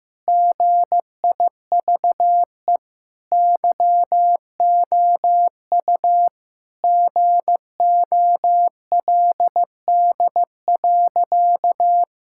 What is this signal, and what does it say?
Morse code, heard as a keyed tone, about 15 wpm, 700 Hz. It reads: GIVE YOU GOLD.
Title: Sentences from Top 1000 Words - Review Part 6 - 15wpm